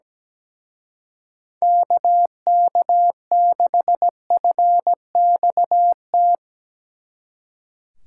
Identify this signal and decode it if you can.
Morse code, heard as a keyed tone, about 18 wpm, 700 Hz. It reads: KK6FXT
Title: Speed Racing - USA Call Signs 17wpm